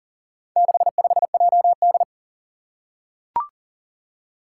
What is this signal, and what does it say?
Morse code, heard as a keyed tone, about 40 wpm, 690 Hz. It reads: 65JD
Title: Four Challenging Letter-Number Combinations 40wpm